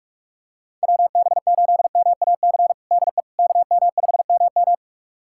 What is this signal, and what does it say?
Morse code, heard as a keyed tone, about 47 wpm, 700 Hz. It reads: WB8MAC DE KM5MM
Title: USA Callsign DE Callsign 45wpm